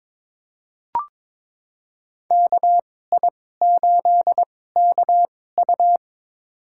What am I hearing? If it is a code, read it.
Morse code, heard as a keyed tone, about 22 wpm, spaced as for 16 wpm, 700 Hz. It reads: KI8KU
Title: USA Call Signs - 2x Character Spacing - 22wpm